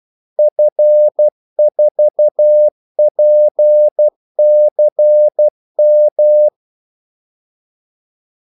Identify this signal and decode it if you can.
Morse code, heard as a keyed tone, about 12 wpm, 600 Hz. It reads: F4PCM
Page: 2 ekstra ordmellemrum mellem kaldesignalerne